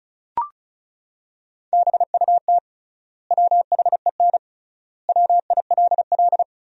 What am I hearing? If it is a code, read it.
Morse code, heard as a keyed tone, about 36 wpm, 700 Hz. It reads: BUT WHEN WILL
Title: 3x Word-Spacing - Sentences from Top 100 Words - 35wpm